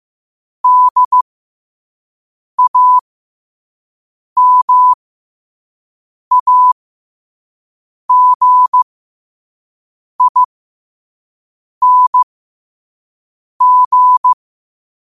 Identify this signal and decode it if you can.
Morse code, heard as a keyed tone, about 15 wpm, spaced as for 5 wpm, 1000 Hz. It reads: DAMAGING
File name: qotdfiles / 5wpm